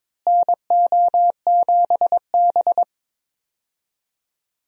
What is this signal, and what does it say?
Morse code, heard as a keyed tone, about 22 wpm, 700 Hz. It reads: NO7B
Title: USA Call Signs - 4 Characters - Repeated 2x - 22wpm